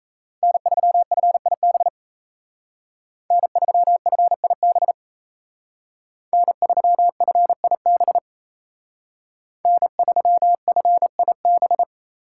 Morse code, heard as a keyed tone, about 36 wpm, 700 Hz. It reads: N3FIB N3FIB N3FIB N3FIB